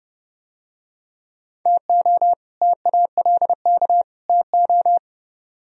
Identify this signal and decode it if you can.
Morse code, heard as a keyed tone, about 30 wpm, 700 Hz. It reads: TO TALK TO